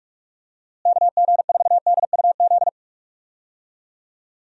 Morse code, heard as a keyed tone, about 46 wpm, 700 Hz. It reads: KG4DUZ
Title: USA Call Signs - 6 Characters - Repeated 3x - 45wpm